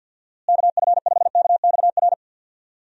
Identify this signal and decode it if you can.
Morse code, heard as a keyed tone, about 52 wpm, 700 Hz. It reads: KF5KXR
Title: USA Call Signs - 6 Characters - Repeated 3x - 50wpm